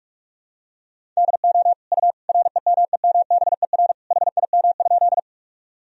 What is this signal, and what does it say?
Morse code, heard as a keyed tone, about 41 wpm, 700 Hz. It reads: DO U REMEMBER HIM?
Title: Sentences from Top 400 Words and Abbreviations 45wpm